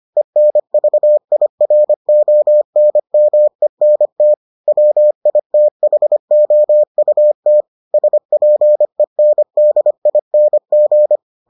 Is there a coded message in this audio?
Yes: Morse code from 0.2 to 11.2 s, heard as a keyed tone, about 25 wpm, 600 Hz. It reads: ENVIRONMENT WITHOUT SPENDING